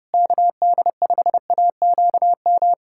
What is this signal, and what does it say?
Morse code, heard as a keyed tone, about 30 wpm, 700 Hz. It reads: KD5AQM
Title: USA Callsign DE Callsign 30wpm